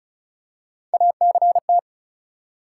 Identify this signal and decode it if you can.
Morse code, heard as a keyed tone, about 34 wpm, 700 Hz. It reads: ACT
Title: Three Letter Words - 35wpm